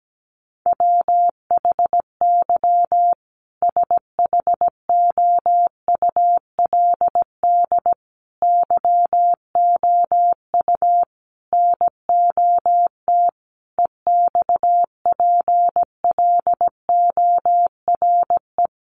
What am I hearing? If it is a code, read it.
Morse code, heard as a keyed tone, about 17 wpm, 700 Hz. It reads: WHY SHOULD YOU NOT EXPLORE
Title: Sentences from Top 1100 Words 17wpm